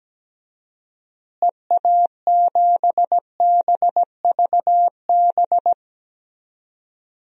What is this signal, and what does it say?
Morse code, heard as a keyed tone, about 17 wpm, 700 Hz. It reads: EA7BVB
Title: DXCC-100 Call Signs - 6 Characters - Repeated 2x - 17wpm